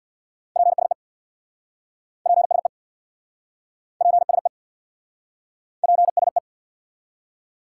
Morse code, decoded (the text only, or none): PSE PSE PSE PSE